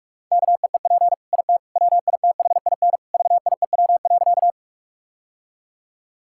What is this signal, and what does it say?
Morse code, heard as a keyed tone, about 46 wpm, 700 Hz. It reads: KEEP IT WITHIN VIEW.
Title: Sentences from Top 1000 Words - Review Part 6 - 45wpm